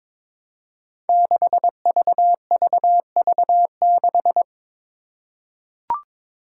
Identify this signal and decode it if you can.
Morse code, heard as a keyed tone, about 21 wpm, 700 Hz. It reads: 6VVV6
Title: Five Challenging Letter-Number Combinations 22wpm